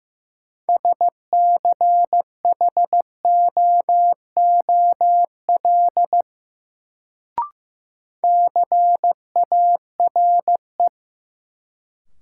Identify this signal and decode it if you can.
Morse code, heard as a keyed tone, about 16 wpm, 700 Hz. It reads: SCHOOL CARE
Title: Sentences from Top 300 Words - New Word Drill - 15wpm